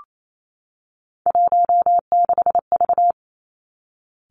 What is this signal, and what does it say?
Morse code, heard as a keyed tone, about 28 wpm, 700 Hz. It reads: 16V
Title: Three Challenging Letter-Number Combinations 28wpm